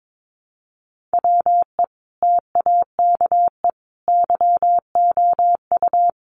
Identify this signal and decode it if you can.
Morse code, heard as a keyed tone, about 22 wpm, 700 Hz. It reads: WE TAKE YOU
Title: A Sound Of Thunder 22wpm